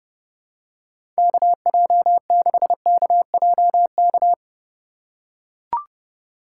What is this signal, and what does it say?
Morse code, heard as a keyed tone, about 30 wpm, 700 Hz. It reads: KJ6KJK
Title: USA Call Signs - 6 Characters - 2x Character Spacing - 30wpm